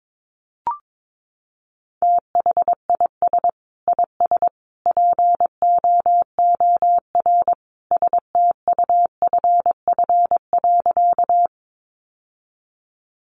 Morse code, heard as a keyed tone, about 22 wpm, 700 Hz. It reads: THIS IS POOR STUFF.
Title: Sentences from Top 1400 Words 22wpm